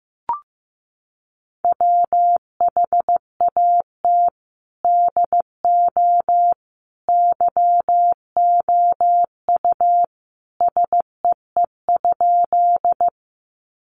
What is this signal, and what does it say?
Morse code, heard as a keyed tone, about 15 wpm, 700 Hz. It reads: WHAT DO YOU SEE?
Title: Sentences from Top 100 Words 15wpm